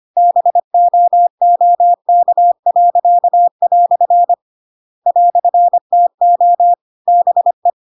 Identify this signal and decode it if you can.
Morse code, heard as a keyed tone, about 25 wpm, 700 Hz. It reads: BOOK." "TO BE